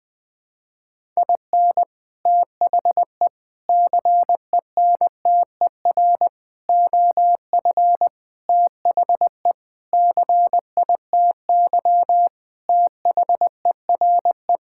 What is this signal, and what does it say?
Morse code, heard as a keyed tone, about 20 wpm, 700 Hz. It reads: IN THE CENTER OF THE CITY THERE
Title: Sentences with 5-Word Phrases - 20wpm